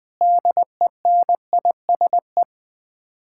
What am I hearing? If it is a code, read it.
Morse code, heard as a keyed tone, about 21 wpm, 700 Hz. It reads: DENISE
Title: Practice QSO - Exchange 1 - Beginning - 20wpm